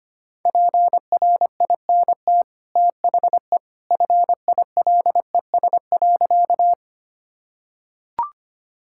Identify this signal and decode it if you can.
Morse code, heard as a keyed tone, about 25 wpm, 690 Hz. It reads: PRINT THE FILES.
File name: Easy Sentences with Hints - 25wpm